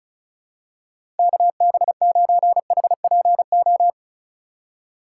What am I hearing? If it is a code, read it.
Morse code, heard as a keyed tone, about 35 wpm, 690 Hz. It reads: KB9HPO
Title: USA Call Signs - 6 Characters - 3x Character Spacing - 35wpm